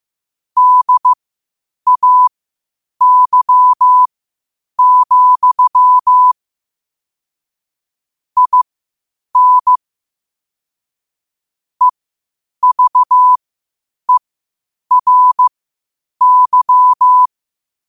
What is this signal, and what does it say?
Morse code, heard as a keyed tone, about 15 wpm, spaced as for 8 wpm, 1000 Hz. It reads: DAY, IN EVERY
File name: qotdfiles / 8wpm